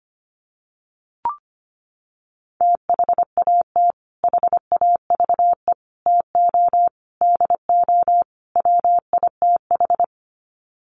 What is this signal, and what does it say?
Morse code, heard as a keyed tone, about 25 wpm, 700 Hz. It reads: THAT HAVE TO DO WITH
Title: Sets of 5 Words Encore2 25wpm